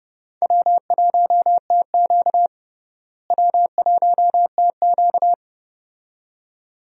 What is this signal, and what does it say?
Morse code, heard as a keyed tone, about 30 wpm, 700 Hz. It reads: W1TQ W1TQ
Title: USA Call Signs - 4 Characters - Repeated 2x - 30wpm